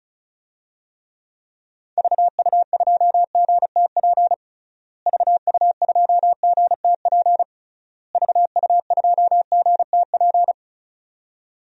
Morse code, heard as a keyed tone, about 34 wpm, 700 Hz. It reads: VU2GTP VU2GTP VU2GTP
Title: DXCC-100 Call Signs - 6 Characters - Repeated 3x - 35wpm